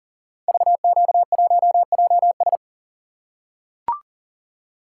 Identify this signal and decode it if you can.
Morse code, heard as a keyed tone, about 38 wpm, 700 Hz. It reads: VQ1JS
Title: Code Groups - 5 Letter:Numbers - 40wpm